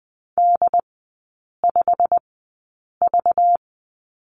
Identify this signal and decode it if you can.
Morse code, heard as a keyed tone, about 20 wpm, spaced as for 8 wpm, 690 Hz. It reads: D5V
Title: 198 Lesson 32e - Letter k - Three Characters 20wpm